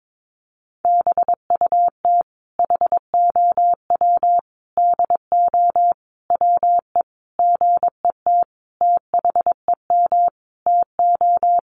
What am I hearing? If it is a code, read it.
Morse code, heard as a keyed tone, about 22 wpm, 700 Hz. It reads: BUT HOW DO WE GET THEM TO
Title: Sentences from Top 700 Words 22wpm